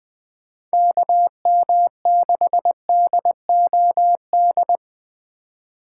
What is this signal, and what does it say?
Morse code, heard as a keyed tone, about 19 wpm, 700 Hz. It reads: KM6DOD
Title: USA Call Signs - 6 Characters - 2x Character Spacing - 20wpm